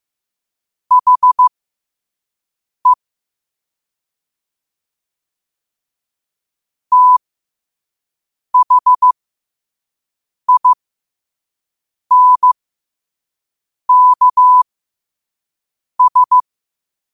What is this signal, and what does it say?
Morse code, heard as a keyed tone, about 15 wpm, spaced as for 5 wpm, 1000 Hz. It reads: HE THINKS
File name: qotdfiles / 5wpm